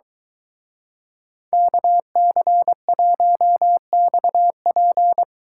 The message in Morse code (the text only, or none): KC1XP